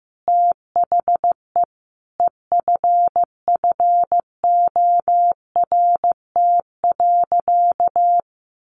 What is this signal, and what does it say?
Morse code, heard as a keyed tone, about 15 wpm, 700 Hz. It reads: THE EFFORT.
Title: Sentences from Top 1200 Words 15wpm